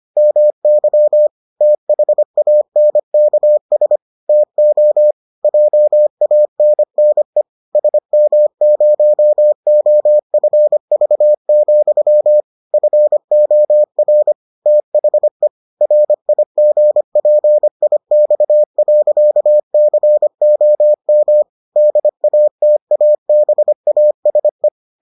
Morse code, heard as a keyed tone, about 25 wpm, 600 Hz. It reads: MY THANKS TO JANNE SM0OFV, FOR THE RIGPIX.COM DATABASE